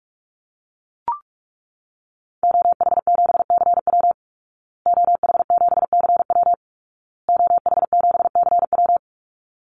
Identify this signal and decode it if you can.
Morse code, heard as a keyed tone, about 44 wpm, 700 Hz. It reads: OH7CW OH7CW OH7CW
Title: Call Signs Repeated 2x - 45wpm